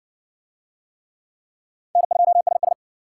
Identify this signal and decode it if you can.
Morse code, heard as a keyed tone, about 56 wpm, 700 Hz. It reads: N2SS